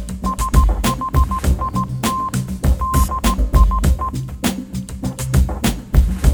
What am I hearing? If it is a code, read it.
Morse code, heard as a keyed tone, about 16 wpm, 1050 Hz. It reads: SHIT DIE
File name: Kirkoid / Kirkoid - Small Stuff (Morse Mix)